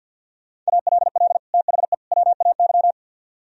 Morse code, heard as a keyed tone, about 62 wpm, 700 Hz. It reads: ALL THE WAY
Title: Speed-Racing - Sets of 3 Words 50wpm